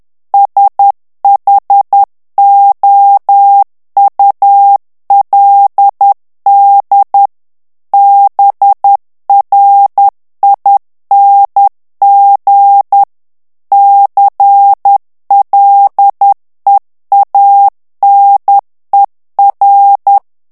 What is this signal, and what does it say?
Morse code, heard as a keyed tone, about 10 wpm, 800 Hz. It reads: SHOULD BRING CLEANER
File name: morse mp3 / ANGLAIS texte clair 2 vitesse 10 mots minute